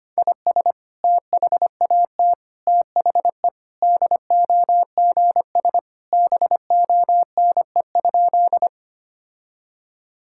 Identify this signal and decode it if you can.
Morse code, heard as a keyed tone, about 25 wpm, 700 Hz. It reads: IS THAT THE DOGS BONE?
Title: Sentences from Top 1600 Words 25wpm